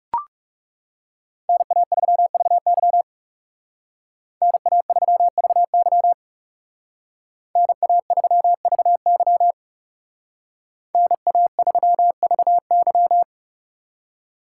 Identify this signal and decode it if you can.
Morse code, heard as a keyed tone, about 37 wpm, 700 Hz. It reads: NA3VY NA3VY NA3VY NA3VY
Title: Speed Racing - USA 2x2 Call Signs 30wpm